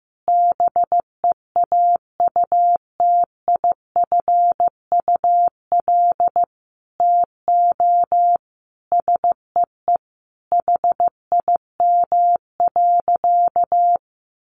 Morse code, heard as a keyed tone, about 15 wpm, 700 Hz. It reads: BEAUTIFUL TO SEE HIM.
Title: Sentences from Top 900 Words 15wpm